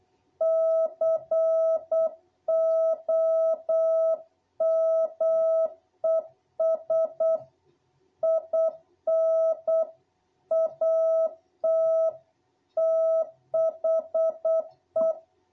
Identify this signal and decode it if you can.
Morse code, heard as a keyed tone, about 8 wpm, 660 Hz. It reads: COMES IN AT THE
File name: jh8cbh / 2021cw